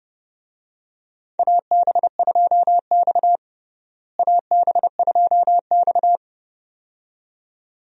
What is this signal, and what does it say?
Morse code, heard as a keyed tone, about 30 wpm, 700 Hz. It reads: AB2X AB2X
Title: USA Call Signs - 4 Characters - Repeated 2x - 30wpm